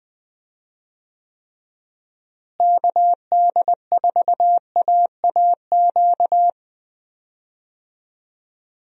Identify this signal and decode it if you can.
Morse code, heard as a keyed tone, about 20 wpm, 700 Hz. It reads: KD4AAQ
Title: USA Call Signs - 6 Characters - Repeated 3x - 20wpm